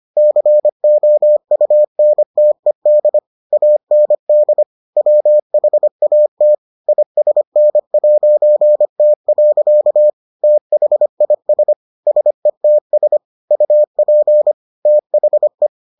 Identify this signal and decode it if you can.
Morse code, heard as a keyed tone, about 25 wpm, 600 Hz. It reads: COUNTED AND WHAT ISN'T. THIS SETS UP THE